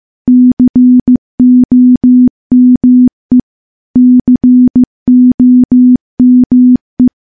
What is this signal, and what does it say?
Morse code, heard as a keyed tone, about 15 wpm, 250 Hz. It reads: COME COME
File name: morse-training-15WPM